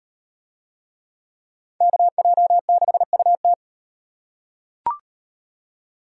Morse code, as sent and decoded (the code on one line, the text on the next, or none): -.- .--- -.... ..- -
KJ6UT